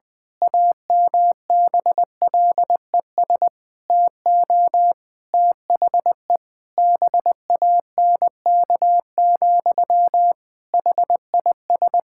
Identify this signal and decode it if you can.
Morse code, heard as a keyed tone, about 20 wpm, 700 Hz. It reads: AMBLES TO THE BANK, HIS
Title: Pangram Challenge v2 - 20wpm